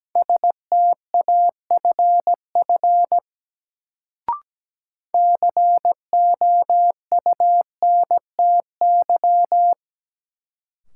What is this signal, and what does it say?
Morse code, heard as a keyed tone, about 17 wpm, 700 Hz. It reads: STAFF COUNTY
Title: Sentences from Top 1200 Words - New Word Drill Alt - 17wpm